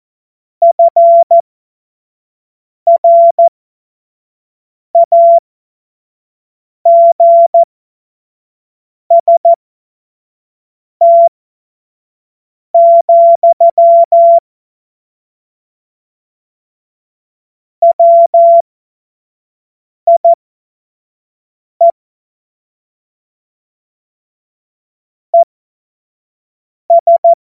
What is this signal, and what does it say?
Morse code, heard as a keyed tone, about 14 wpm, spaced as for 5 wpm, 680 Hz. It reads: FRAGST, WIE ES
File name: Morsekurs / Witze / Witz 10. Mai 2022 25 F70